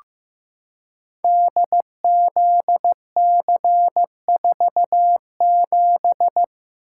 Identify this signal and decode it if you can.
Morse code, heard as a keyed tone, about 16 wpm, 700 Hz. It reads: DZC47
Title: Code Groups - 5 Letter:Numbers - 15wpm